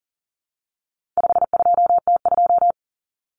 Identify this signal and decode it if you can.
Morse code, heard as a keyed tone, about 40 wpm, 700 Hz. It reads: 52T2
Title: Common Cut Numbers - 4-Digit 40wpm